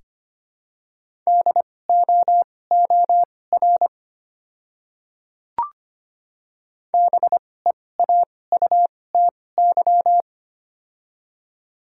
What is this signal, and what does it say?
Morse code, heard as a keyed tone, about 25 wpm, spaced as for 18 wpm, 700 Hz. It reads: DOOR BEAUTY
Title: 2x Character Spacing - Top 500 Words - 25wpm